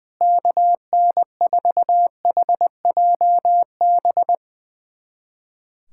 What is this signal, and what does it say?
Morse code, heard as a keyed tone, about 20 wpm, 700 Hz. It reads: KN4HJB
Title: USA Callsign DE Callsign 20wpm